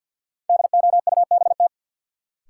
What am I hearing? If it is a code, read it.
Morse code, heard as a keyed tone, about 52 wpm, 700 Hz. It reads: DOUBT
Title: Sentences from Top 1500 Words - New Word Drill Alt - 50wpm